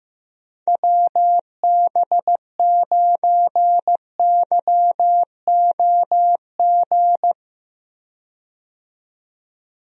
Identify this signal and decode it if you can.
Morse code, heard as a keyed tone, about 16 wpm, 700 Hz. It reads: WB9YOG
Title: USA Call Signs - 6 Characters - 15wpm